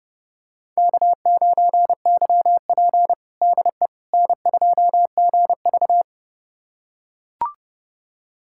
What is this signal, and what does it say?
Morse code, heard as a keyed tone, about 30 wpm, 700 Hz. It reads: K9YP DE N2GV
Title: USA Callsign DE Callsign - 4 Characters - 30wpm